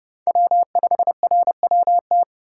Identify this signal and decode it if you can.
Morse code, heard as a keyed tone, about 32 wpm, 700 Hz. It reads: W5RWT
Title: USA Call Signs - 5 Characters - 2x Character Spacing - 30wpm